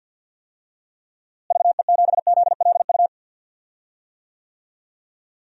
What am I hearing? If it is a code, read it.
Morse code, heard as a keyed tone, about 51 wpm, 700 Hz. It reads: VE7ZLU
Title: DXCC-100 Call Signs - 6 Characters - Repeated 3x - 50wpm